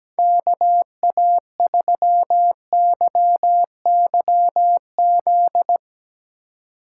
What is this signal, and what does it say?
Morse code, heard as a keyed tone, about 17 wpm, 700 Hz. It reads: KA3YYZ
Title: USA Call Signs - 6 Characters - 3x Character Spacing - 17wpm